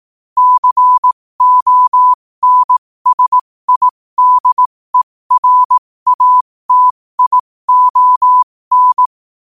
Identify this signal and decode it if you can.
Morse code, heard as a keyed tone, about 18 wpm, spaced as for 15 wpm, 1000 Hz. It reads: CONSIDERATION
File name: qotdfiles / 15wpm